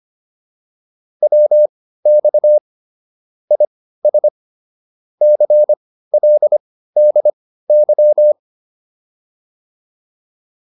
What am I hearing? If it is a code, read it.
Morse code, heard as a keyed tone, about 25 wpm, spaced as for 15 wpm, 600 Hz. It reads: WX IS CLDY
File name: PR302 15